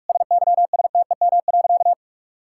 Morse code, heard as a keyed tone, about 46 wpm, 700 Hz. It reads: SYSTEM.
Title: Sentences from Top 1000 Words - Review Part 6 - 45wpm